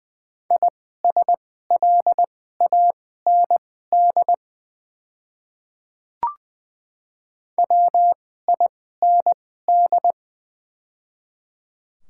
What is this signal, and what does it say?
Morse code, heard as a keyed tone, about 20 wpm, spaced as for 14 wpm, 700 Hz. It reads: ISLAND WIND
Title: 2x Character Spacing - Top 500 Words - 20wpm